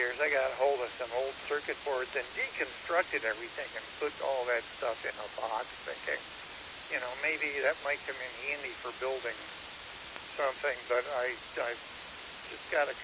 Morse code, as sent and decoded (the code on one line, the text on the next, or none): none